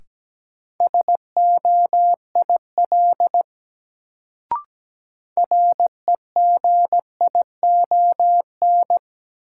Morse code, heard as a keyed tone, about 17 wpm, 700 Hz. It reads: SOIL REGION
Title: Sentences from Top 1500 Words - New Word Drill Alt - 17wpm